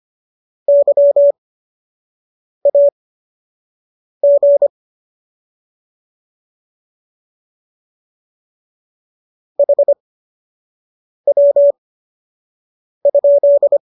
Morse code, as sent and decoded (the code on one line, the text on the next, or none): -.-- .- --. / .... .-- ..--..
YAG HW?